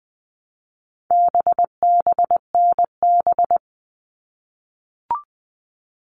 Morse code, heard as a keyed tone, about 20 wpm, 700 Hz. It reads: BBNB
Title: All Cut Numbers - 4-Digit 20wpm